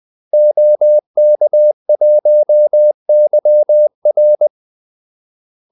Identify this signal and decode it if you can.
Morse code, heard as a keyed tone, about 20 wpm, 600 Hz. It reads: OK1YR
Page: Kaldesignaler 20 wpm